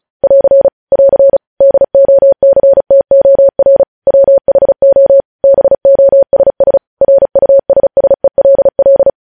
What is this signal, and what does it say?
Morse code, heard as a keyed tone, about 36 wpm, 550 Hz. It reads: <AR> <AR> DOCTOR WHO BOSS RUSSELL
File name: podcast / News headlines-1x-35x35